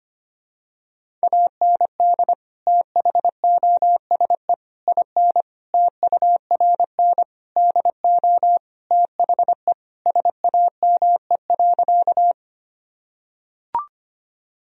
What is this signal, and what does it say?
Morse code, heard as a keyed tone, about 25 wpm, 700 Hz. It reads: AND THOSE IN TURN DO THE SAME.
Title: Sentences from Top 1000 Words - Review Part 1 - 25wpm